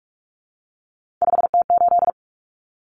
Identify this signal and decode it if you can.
Morse code, heard as a keyed tone, about 44 wpm, 700 Hz. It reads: 5T8
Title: Common Cut Numbers - 3-Digit 45wpm